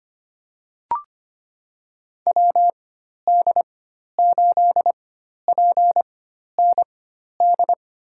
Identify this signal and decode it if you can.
Morse code, heard as a keyed tone, about 25 wpm, spaced as for 12 wpm, 700 Hz. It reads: WD8PND
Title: USA Call Signs - 6 Characters - 4x Character Spacing - 25wpm